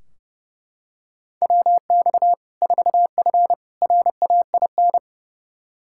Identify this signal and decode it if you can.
Morse code, heard as a keyed tone, about 30 wpm, 700 Hz. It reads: WX 4F RAIN
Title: Practice QSO - Exchange 2 - Info - 30wpm